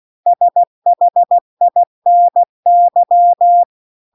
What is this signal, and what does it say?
Morse code, heard as a keyed tone, about 16 wpm, 700 Hz. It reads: SHINY